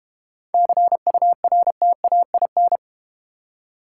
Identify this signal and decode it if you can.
Morse code, heard as a keyed tone, about 32 wpm, 700 Hz. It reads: CURTAIN